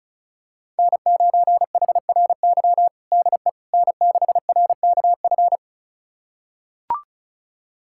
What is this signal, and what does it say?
Morse code, heard as a keyed tone, about 36 wpm, 700 Hz. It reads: N9HRY DE N6RKF
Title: USA Callsign DE Callsign - 5 Characters - 35wpm